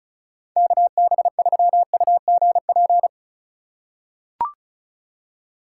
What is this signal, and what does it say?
Morse code, heard as a keyed tone, about 36 wpm, 700 Hz. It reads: KB3UGP
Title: USA Call Signs - 8x Character Spacing - 35wpm